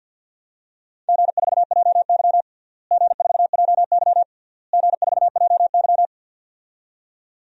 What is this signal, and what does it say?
Morse code, heard as a keyed tone, about 48 wpm, 700 Hz. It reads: G4JY G4JY G4JY
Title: DXCC-100 Call Signs - 4 Characters - Repeated 3x - 50wpm